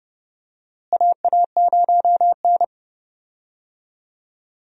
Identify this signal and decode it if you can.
Morse code, heard as a keyed tone, about 30 wpm, 700 Hz. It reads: AA0N
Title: USA Call Signs - 4 Characters - Repeated 2x - 30wpm